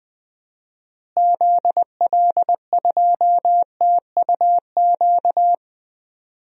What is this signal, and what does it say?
Morse code, heard as a keyed tone, about 20 wpm, 700 Hz. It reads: ZL2TUQ
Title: DXCC-100 Call Signs - 6 Characters - Repeated 2x - 20wpm